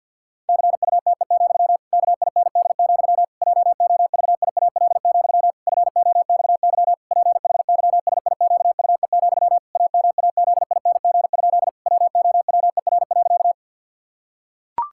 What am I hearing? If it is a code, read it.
Morse code, heard as a keyed tone, about 50 wpm, 700 Hz. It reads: CUTE, KIND, JOVIAL, FOXY PHYSIQUE, AMAZING? WOWER.
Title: Pangram Challenge v2 - 50wpm